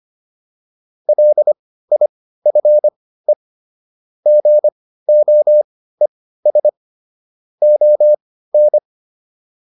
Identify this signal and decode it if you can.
Morse code, heard as a keyed tone, about 25 wpm, spaced as for 15 wpm, 600 Hz. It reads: LIFE GOES ON
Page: intermediate-practice-files